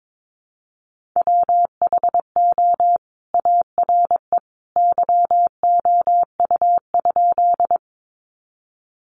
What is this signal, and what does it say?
Morse code, heard as a keyed tone, about 22 wpm, 700 Hz. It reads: WHO ARE YOU?
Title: Sentences from Top 100 Words 22wpm